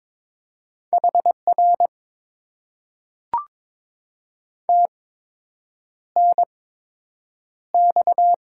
Morse code, heard as a keyed tone, about 22 wpm, spaced as for 8 wpm, 700 Hz. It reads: HR TNX
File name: Top 90 QSO Elements - 8x Character Spacing - 22wpm